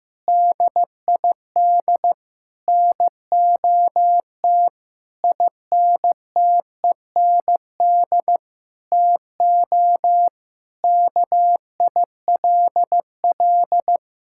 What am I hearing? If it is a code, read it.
Morse code, heard as a keyed tone, about 15 wpm, 700 Hz. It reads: DID NOT INTEND TO KILL
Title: Sentences from Top 1600 Words 15wpm